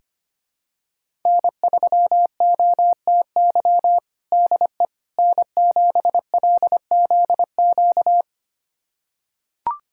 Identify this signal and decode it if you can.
Morse code, heard as a keyed tone, about 25 wpm, 700 Hz. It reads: N3OTY DE N7LZQ
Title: USA Callsign DE Callsign - 5 Characters - 25wpm